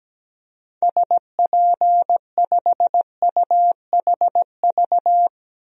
Morse code, heard as a keyed tone, about 17 wpm, 700 Hz. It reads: SP5UHV
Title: DXCC-100 Call Signs - 6 Characters - 17wpm